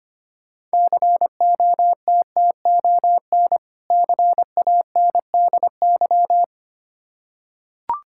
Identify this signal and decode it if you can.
Morse code, heard as a keyed tone, about 25 wpm, 690 Hz. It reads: COTTON CANDY
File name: practice / 25wpm / Difficult Words with Hints - 25wpm